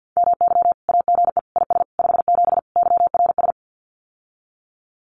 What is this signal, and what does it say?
Morse code, heard as a keyed tone, about 50 wpm, 700 Hz. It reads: MY AGE IS 57 YRS
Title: Common QSO Phrases Encore 50wpm